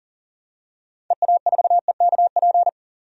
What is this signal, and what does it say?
Morse code, heard as a keyed tone, about 40 wpm, 700 Hz. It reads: EA4EKP
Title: DXCC-100 Call Signs - 6 Characters - 40wpm